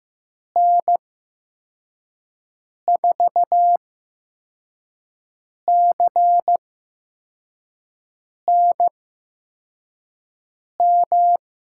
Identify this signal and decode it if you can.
Morse code, heard as a keyed tone, about 15 wpm, spaced as for 4 wpm, 700 Hz. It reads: N4CNM
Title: USA Call Signs - 6 Characters - 8x Character Spacing - 15wpm